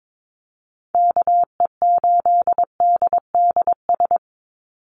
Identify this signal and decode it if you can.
Morse code, heard as a keyed tone, about 22 wpm, 700 Hz. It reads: KE8DDS
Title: USA Call Signs 22wpm